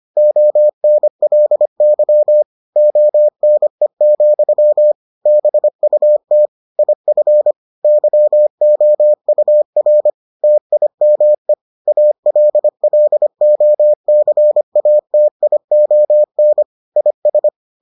Morse code, heard as a keyed tone, about 25 wpm, 600 Hz. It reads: ONLY ONE, BUT IF YOUR TIME ALLOCATION IS